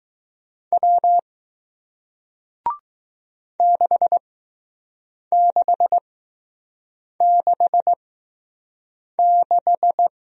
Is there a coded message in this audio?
Yes: Morse code from 0.7 to 10.1 s, heard as a keyed tone, about 20 wpm, spaced as for 6 wpm, 700 Hz. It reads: W 6666